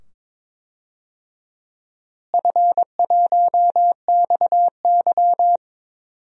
Morse code, heard as a keyed tone, about 22 wpm, 700 Hz. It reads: F1XY
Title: DXCC-100 Call Signs - 4 Characters - Repeated 2x - 22wpm